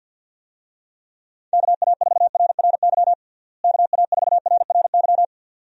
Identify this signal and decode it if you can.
Morse code, heard as a keyed tone, about 53 wpm, 700 Hz. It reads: KA4RRY KA4RRY
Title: USA Call Signs - 6 Characters - Repeated 2x - 50wpm